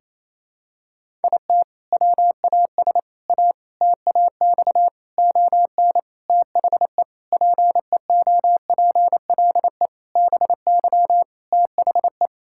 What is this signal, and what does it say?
Morse code, heard as a keyed tone, about 28 wpm, 700 Hz. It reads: IT WAS A TAX ON THE PEOPLE BY THE
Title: Sentences from Top 1000 Words - Review Part 5 - 28wpm